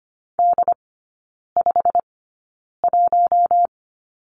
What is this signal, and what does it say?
Morse code, heard as a keyed tone, about 25 wpm, spaced as for 9 wpm, 700 Hz. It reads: D51
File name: Lesson 15e - Number 5 - Three Characters 25wpm